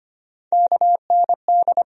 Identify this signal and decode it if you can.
Morse code, heard as a keyed tone, about 25 wpm, 700 Hz. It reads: KND
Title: USA Call Sign Postfixes - 3 Letters 25wpm